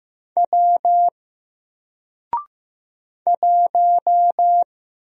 Morse code, heard as a keyed tone, about 15 wpm, 700 Hz. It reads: W 1
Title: Single Letter-Number Alt 15wpm